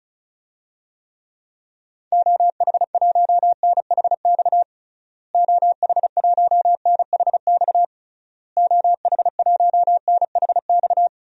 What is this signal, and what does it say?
Morse code, heard as a keyed tone, about 36 wpm, 700 Hz. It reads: OH1NHX OH1NHX OH1NHX
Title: DXCC-100 Call Signs - 6 Characters - Repeated 3x - 35wpm